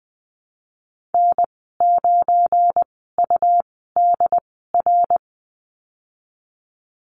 Morse code, heard as a keyed tone, about 20 wpm, spaced as for 15 wpm, 700 Hz. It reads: N9UDR
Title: 219 Lesson 35g - Number 0 - Call Signs 20wpm